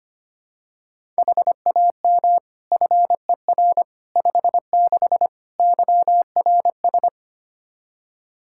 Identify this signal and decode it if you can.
Morse code, heard as a keyed tone, about 25 wpm, 700 Hz. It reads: HAM FER 56 YRS
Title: K1USN SST QSO Practice 25wpm